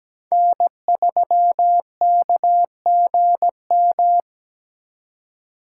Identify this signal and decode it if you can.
Morse code, heard as a keyed tone, about 20 wpm, 700 Hz. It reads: N3KGM
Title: Speed Racing - USA 1x3 Call Signs 17wpm